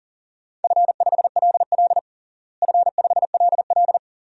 Morse code, heard as a keyed tone, about 40 wpm, 700 Hz. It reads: F5LL F5LL
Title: DXCC-100 Call Signs - 4 Characters - Repeated 2x - 40wpm